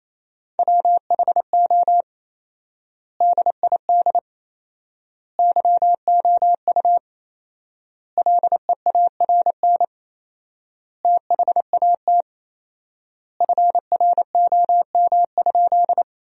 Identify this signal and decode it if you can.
Morse code, heard as a keyed tone, about 28 wpm, 700 Hz. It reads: WHO DID YOU LEARN THAT FROM?
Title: 4x Word-Spacing - Sentences from Top 300 Words - 28wpm